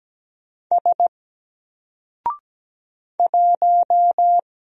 Morse code, heard as a keyed tone, about 17 wpm, 700 Hz. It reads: S 1
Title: Single Letter-Number Alt 17wpm